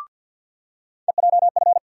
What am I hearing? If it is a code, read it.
Morse code, heard as a keyed tone, about 47 wpm, 700 Hz. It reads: EJF
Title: USA Call Sign Postfixes - 3 Letters 50wpm